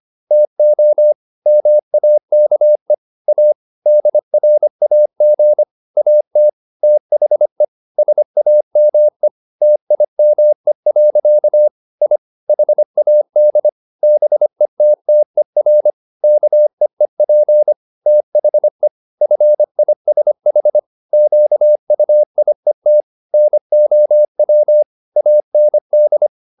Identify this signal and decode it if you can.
Morse code, heard as a keyed tone, about 25 wpm, 600 Hz. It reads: TO MAKE A DRAG AT THE SAME TIME. I HAD BETTER KEEP THE FISH QUIET NOW AND